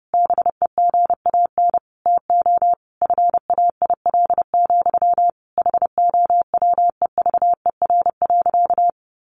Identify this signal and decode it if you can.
Morse code, heard as a keyed tone, about 30 wpm, 700 Hz. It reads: BEGAN TO FAIL, HOWEVER.
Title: Sentences from Top 1100 Words 30wpm